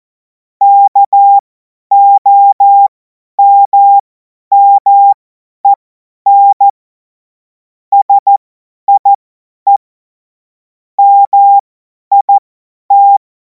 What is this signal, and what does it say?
Morse code, heard as a keyed tone, about 14 wpm, spaced as for 10 wpm, 800 Hz. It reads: KOMMEN SIE MIT
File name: Morsekurs / Witze / Witz 24. Februar 2026 50 F70